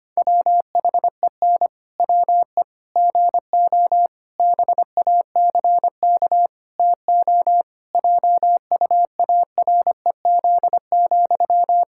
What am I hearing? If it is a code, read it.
Morse code, heard as a keyed tone, about 25 wpm, 700 Hz. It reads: WHEN WE GO BACK TO JUAREZ,
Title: Pangram Challenge v2 - 25wpm